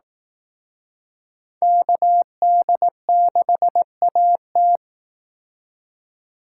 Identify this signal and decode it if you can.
Morse code, heard as a keyed tone, about 18 wpm, 700 Hz. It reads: KD6AT
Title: Speed Racing - USA 2x2 Call Signs 15wpm